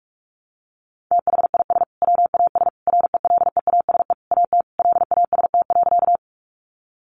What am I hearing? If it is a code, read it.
Morse code, heard as a keyed tone, about 44 wpm, 700 Hz. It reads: THIS WAS RELEASE AT LAST.
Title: Sentences from Top 1100 Words 45wpm